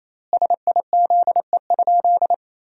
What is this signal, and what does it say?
Morse code, heard as a keyed tone, about 28 wpm, 700 Hz. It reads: SIZE?
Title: 4x Word-Spacing - Sentences from Top 500 Words - 28wpm